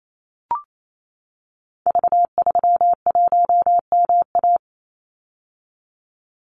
Morse code, heard as a keyed tone, about 28 wpm, 700 Hz. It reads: V31MA
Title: Call Signs - 28wpm